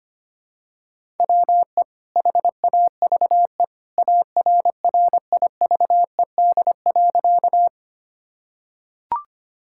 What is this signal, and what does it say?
Morse code, heard as a keyed tone, about 25 wpm, 700 Hz. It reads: WE HAVE ARRIVED.
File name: practice / 25wpm / Sentences with 2-Word Phrases - 25wpm